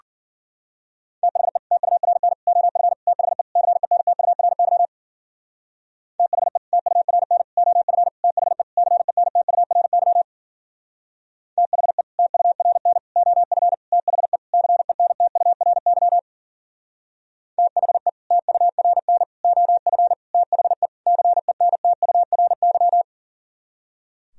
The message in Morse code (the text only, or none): THE TURN OF THE CENTURY THE TURN OF THE CENTURY THE TURN OF THE CENTURY THE TURN OF THE CENTURY